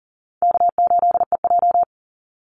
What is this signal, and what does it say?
Morse code, heard as a keyed tone, about 42 wpm, 700 Hz. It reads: K8EJ
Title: Easy Call Signs - 40wpm